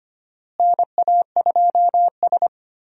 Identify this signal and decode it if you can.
Morse code, heard as a keyed tone, about 25 wpm, 700 Hz. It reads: NA2S
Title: USA Call Signs - 4 Characters - Repeated 3x - 25wpm